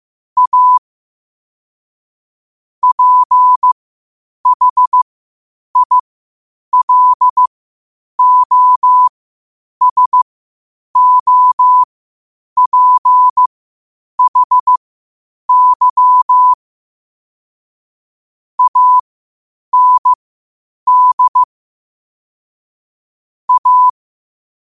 Morse code, heard as a keyed tone, about 15 wpm, spaced as for 8 wpm, 1000 Hz. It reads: A PHILOSOPHY AND A
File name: qotdfiles / 8wpm